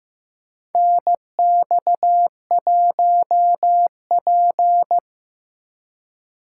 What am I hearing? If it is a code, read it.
Morse code, heard as a keyed tone, about 15 wpm, 700 Hz. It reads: NX1P
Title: USA Call Signs - 4 Characters - 6x Character Spacing - 15wpm